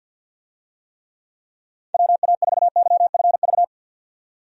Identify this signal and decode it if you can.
Morse code, heard as a keyed tone, about 52 wpm, 700 Hz. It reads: WA4YFV
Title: USA Call Signs - 6 Characters - 50wpm